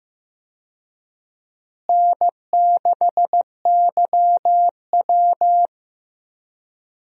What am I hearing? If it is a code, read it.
Morse code, heard as a keyed tone, about 15 wpm, 700 Hz. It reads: N6YW
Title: USA Call Signs - 4 Characters - Repeated 2x - 15wpm